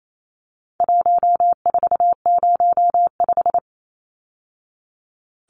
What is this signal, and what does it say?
Morse code, heard as a keyed tone, about 28 wpm, 700 Hz. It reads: 1405
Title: Numbers - 4-Digit 28wpm